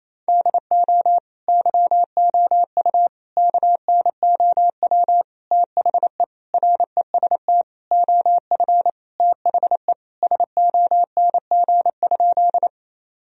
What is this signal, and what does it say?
Morse code, heard as a keyed tone, about 28 wpm, 700 Hz. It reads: DO YOU KNOW THE REST OF THE SONG?
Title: Sentences from Top 1000 Words - Review Part 2 - 28wpm